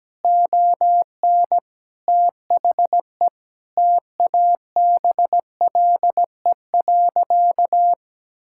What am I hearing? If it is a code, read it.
Morse code, heard as a keyed tone, about 17 wpm, 700 Hz. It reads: ON THE TABLE.
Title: Sentences from Top 1000 Words - Review Part 5 - 17wpm